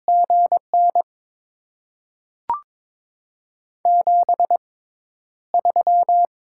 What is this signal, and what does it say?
Morse code, heard as a keyed tone, about 22 wpm, spaced as for 11 wpm, 700 Hz. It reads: GN 73
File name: Top 20 QSO Elements - 6x Character Spacing - 22wpm